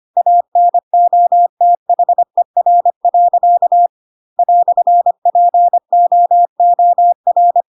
Morse code, heard as a keyed tone, about 25 wpm, 700 Hz. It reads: ANOTHER. "POOR
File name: Pinocchio / Pinocchio ch3 25 25 ik1hge